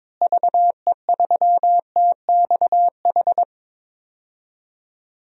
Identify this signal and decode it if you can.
Morse code, heard as a keyed tone, about 22 wpm, 700 Hz. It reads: VE3TXH
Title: DXCC-100 Call Signs - 6 Characters - Repeated 3x - 22wpm